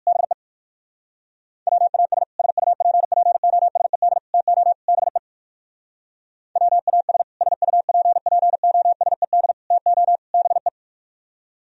Morse code, heard as a keyed tone, about 58 wpm, 700 Hz. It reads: BE WAS SUPPOSED TO BE WAS SUPPOSED TO BE